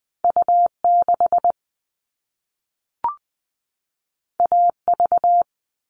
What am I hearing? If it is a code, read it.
Morse code, heard as a keyed tone, about 20 wpm, 700 Hz. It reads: U6 AV
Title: All Cut Numbers - 2-Digit 20wpm